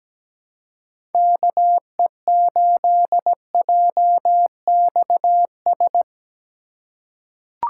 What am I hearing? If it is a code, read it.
Morse code, heard as a keyed tone, about 17 wpm, 700 Hz. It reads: KE8JXS
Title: USA Call Signs - 6 Characters - 4x Character Spacing - 17wpm